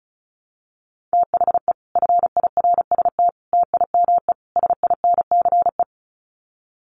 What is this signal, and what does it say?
Morse code, heard as a keyed tone, about 34 wpm, 700 Hz. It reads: THE FIRST TIME SINCE
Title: Sets of 4 Words 35wpm